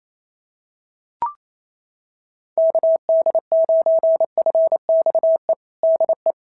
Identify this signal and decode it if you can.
Morse code, heard as a keyed tone, about 28 wpm, 660 Hz. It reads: KD9FXE DE
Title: SOTA QSO Practice 28wpm